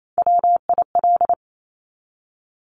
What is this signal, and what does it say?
Morse code, heard as a keyed tone, about 28 wpm, 700 Hz. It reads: WIL
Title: Top Three Letter Combinations in Words 28wpm